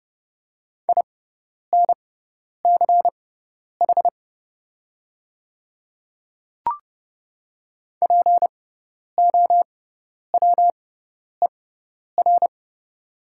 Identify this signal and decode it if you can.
Morse code, heard as a keyed tone, about 30 wpm, spaced as for 10 wpm, 690 Hz. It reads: INCH POWER
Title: 6x Character Spacing - Top 500 Words - 30wpm